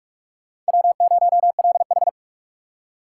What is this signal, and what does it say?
Morse code, heard as a keyed tone, about 58 wpm, 700 Hz. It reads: W0LH